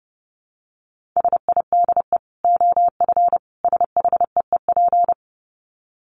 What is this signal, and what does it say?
Morse code, heard as a keyed tone, about 30 wpm, 700 Hz. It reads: SIDE OF SHEEP
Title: Dit Challenge 30wpm